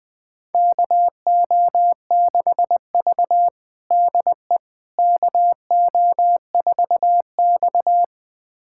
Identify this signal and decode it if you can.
Morse code, heard as a keyed tone, about 20 wpm, 700 Hz. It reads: KO6V DE KO4X
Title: USA Callsign DE Callsign - 4 Characters - 20wpm